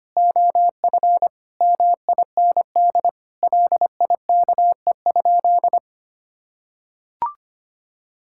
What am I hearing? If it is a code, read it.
Morse code, heard as a keyed tone, about 25 wpm, 700 Hz. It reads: OF MIND LIKE?
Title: Sentences from Top 1000 Words - Review Part 6 - 25wpm